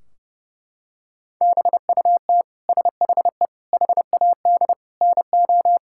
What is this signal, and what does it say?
Morse code, heard as a keyed tone, about 30 wpm, 700 Hz. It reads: BUT SHE HAD NO
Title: Sentences from Top 1600 Words 30wpm